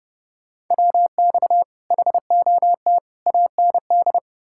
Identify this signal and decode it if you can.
Morse code, heard as a keyed tone, about 40 wpm, 700 Hz. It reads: WX HOT AND